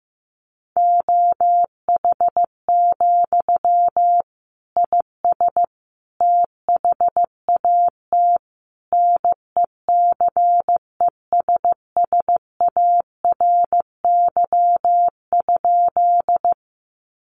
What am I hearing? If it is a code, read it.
Morse code, heard as a keyed tone, about 15 wpm, 700 Hz. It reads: OH, IS THAT NECESSARY?
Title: Sentences from Top 800 Words 15wpm